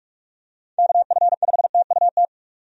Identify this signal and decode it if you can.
Morse code, heard as a keyed tone, about 44 wpm, 700 Hz. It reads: KF5TUT
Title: USA Call Signs - 6 Characters - 45wpm